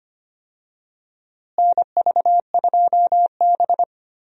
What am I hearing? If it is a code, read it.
Morse code, heard as a keyed tone, about 25 wpm, 700 Hz. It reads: NV2B
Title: USA Call Signs - 4 Characters - Repeated 2x - 25wpm